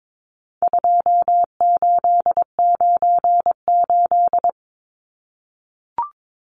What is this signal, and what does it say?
Morse code, heard as a keyed tone, about 22 wpm, 700 Hz. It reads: 2898
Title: Numbers - 4-Digit 22wpm